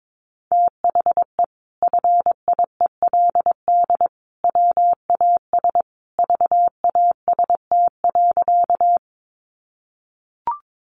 Morse code, heard as a keyed tone, about 22 wpm, 700 Hz. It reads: THE FIELD WAS VAST.
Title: Sentences from Top 1300 Words 22wpm